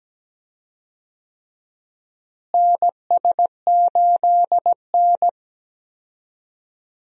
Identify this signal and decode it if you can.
Morse code, heard as a keyed tone, about 17 wpm, 700 Hz. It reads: NS8N
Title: USA Call Signs - 4 Characters - Repeated 3x - 17wpm